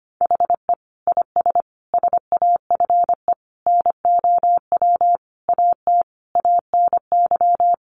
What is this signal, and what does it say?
Morse code, heard as a keyed tone, about 25 wpm, 700 Hz. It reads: HE IS SAFE NOW AT ANY
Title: Sentences from Top 900 Words 25wpm